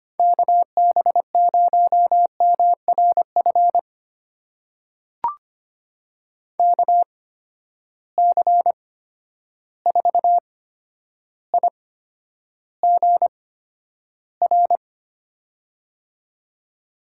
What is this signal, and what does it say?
Morse code, heard as a keyed tone, about 25 wpm, spaced as for 11 wpm, 700 Hz. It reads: KB0MRF KC4IGR
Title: USA Call Signs - 6 Characters - 8x Character Spacing - 25wpm